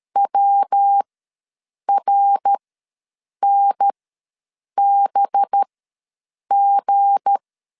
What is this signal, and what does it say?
Morse code, heard as a keyed tone, about 13 wpm, spaced as for 7 wpm, 800 Hz. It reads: WRNBG
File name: LEZIONI / 10 SIGNS / 10C DOT ws 12wpm 3548